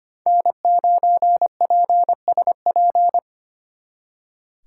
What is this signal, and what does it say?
Morse code, heard as a keyed tone, about 31 wpm, 700 Hz. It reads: N9PSP